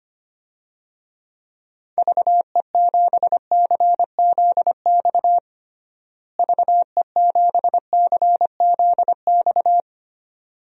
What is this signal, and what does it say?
Morse code, heard as a keyed tone, about 25 wpm, 700 Hz. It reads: VE7CZX VE7CZX
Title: DXCC-100 Call Signs - 6 Characters - Repeated 2x - 25wpm